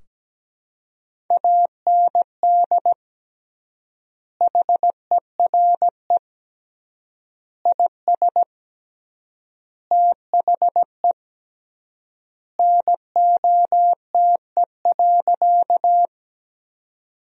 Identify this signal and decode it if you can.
Morse code, heard as a keyed tone, about 17 wpm, 700 Hz. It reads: AND HERE IS THE NOTE.
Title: 3x Word-Spacing - Sentences from Top 500 Words - 17wpm